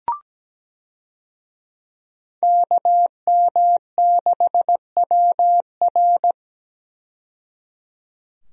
Morse code, heard as a keyed tone, about 17 wpm, 700 Hz. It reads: KM6WR
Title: USA Call Signs - 5 Characters - 17wpm